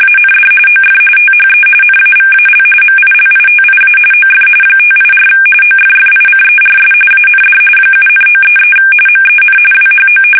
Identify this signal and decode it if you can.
RTTY, about 45.45 baud. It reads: AND OF THE NAVAL MILITIA OF 584 OFFICERS AND 7,933 MEN WHO
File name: fsk850